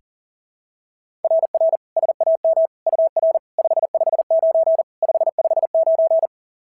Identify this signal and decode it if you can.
Morse code, heard as a keyed tone, about 40 wpm, 690 Hz. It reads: RR SAM UR 559 559
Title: SOTA QSO Practice 40wpm